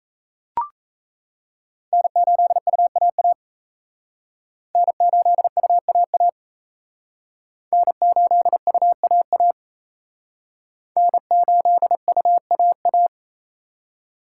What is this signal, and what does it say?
Morse code, heard as a keyed tone, about 36 wpm, 700 Hz. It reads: N8UAA N8UAA N8UAA N8UAA